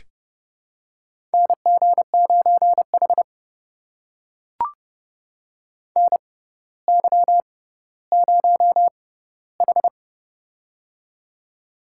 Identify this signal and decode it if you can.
Morse code, heard as a keyed tone, about 30 wpm, spaced as for 15 wpm, 700 Hz. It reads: NG9H NY0H
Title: USA Call Signs - 4 Characters - 6x Character Spacing - 30wpm